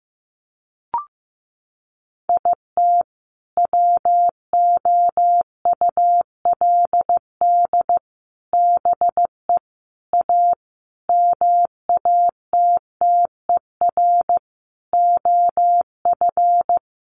Morse code, heard as a keyed tone, about 15 wpm, 700 Hz. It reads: IT WOULD BE A MATTER OF
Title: Sentences from Top 1400 Words 15wpm